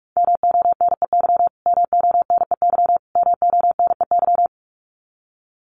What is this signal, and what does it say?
Morse code, heard as a keyed tone, about 44 wpm, 700 Hz. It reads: MONEY MONEY MONEY
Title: Top 500 Words Repeated 45wpm